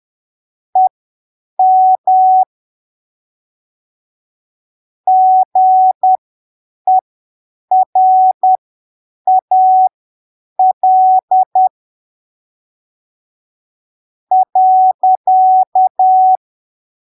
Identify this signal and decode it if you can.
Morse code, heard as a keyed tone, about 10 wpm, spaced as for 7 wpm, 750 Hz. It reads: EM GERAL .
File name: arquivos / texto / texto 01 10ppm